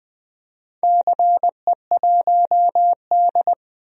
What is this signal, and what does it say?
Morse code, heard as a keyed tone, about 20 wpm, 700 Hz. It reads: CE1D
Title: DXCC-100 Call Signs - 4 Characters - Repeated 2x - 20wpm